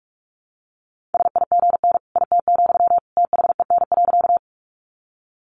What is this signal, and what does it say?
Morse code, heard as a keyed tone, about 44 wpm, 700 Hz. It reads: SIGN IT, THEN.
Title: Sentences from Top 1000 Words 45wpm